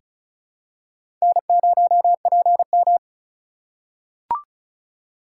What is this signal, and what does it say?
Morse code, heard as a keyed tone, about 36 wpm, 700 Hz. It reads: N0PM
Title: USA Call Signs - 4 Characters - 3x Character Spacing - 35wpm